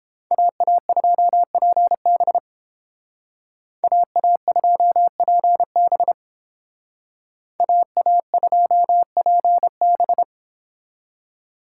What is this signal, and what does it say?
Morse code, heard as a keyed tone, about 29 wpm, 700 Hz. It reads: AA2PB AA2PB AA2PB